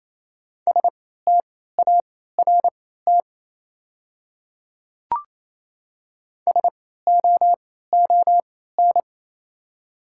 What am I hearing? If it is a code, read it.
Morse code, heard as a keyed tone, about 28 wpm, spaced as for 16 wpm, 700 Hz. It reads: START SOON
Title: 3x Character Spacing - Top 500 Words - 28wpm